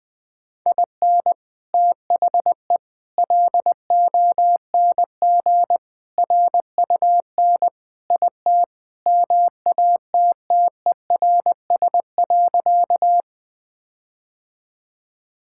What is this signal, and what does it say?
Morse code, heard as a keyed tone, about 20 wpm, 700 Hz. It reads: IN THE LONG RUN IT MATTERS.
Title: Sentences with 4-Word Phrases - Spoken Beginning - 20wpm